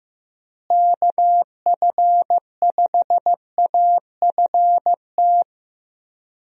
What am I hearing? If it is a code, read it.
Morse code, heard as a keyed tone, about 16 wpm, 700 Hz. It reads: KF5AFT
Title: USA Call Signs - 6 Characters - Repeated 3x - 15wpm